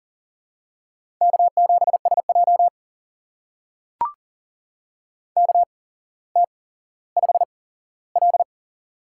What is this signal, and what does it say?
Morse code, heard as a keyed tone, about 43 wpm, spaced as for 17 wpm, 700 Hz. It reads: K7SJ KT5L
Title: USA Call Signs - 4 Characters - 8x Character Spacing - 40wpm